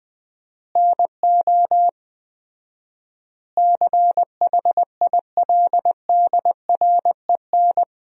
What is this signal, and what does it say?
Morse code, heard as a keyed tone, about 19 wpm, 700 Hz. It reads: NO CHILDREN
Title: 4x Word-Spacing - Sentences from Top 400 Words - 20wpm